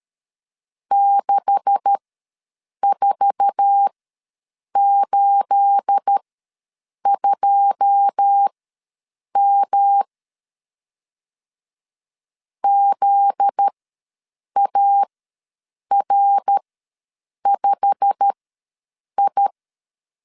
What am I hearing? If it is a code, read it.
Morse code, heard as a keyed tone, about 13 wpm, spaced as for 7 wpm, 800 Hz. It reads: 6482M ZAR5I
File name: LEZIONI / 9 286 ws 12wpm 3351